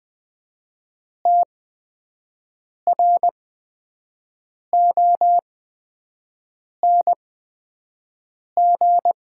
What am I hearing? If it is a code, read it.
Morse code, heard as a keyed tone, about 20 wpm, spaced as for 5 wpm, 700 Hz. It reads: TRONG
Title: 8x Character Spacing - Top 500 Words - 20wpm